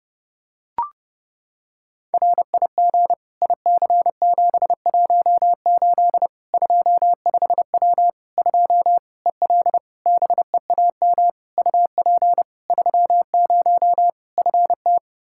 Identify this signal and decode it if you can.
Morse code, heard as a keyed tone, about 32 wpm, 700 Hz. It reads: RIG IC718 25W 2 EL BEAM UP 30 FT
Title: Practice QSO - Exchange 2 - Info - 30wpm